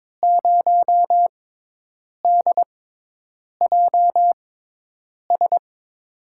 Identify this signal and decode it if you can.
Morse code, heard as a keyed tone, about 22 wpm, spaced as for 8 wpm, 700 Hz. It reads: 0DJS
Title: USA Call Signs - 6 Characters - 6x Character Spacing - 22wpm